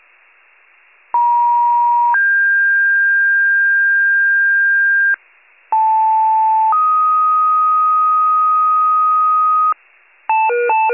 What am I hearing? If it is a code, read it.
Two-tone page, heard as a one-second tone, then a three-second tone, about 950 then 1600 Hz, about 880 then 1250 Hz, with noise. It is heard from 1.1 to 9.7 s.
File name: wcalert